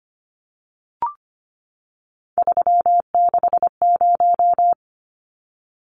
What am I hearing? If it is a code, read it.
Morse code, heard as a keyed tone, about 25 wpm, 700 Hz. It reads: 360
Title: Numbers - 3-Digit 25wpm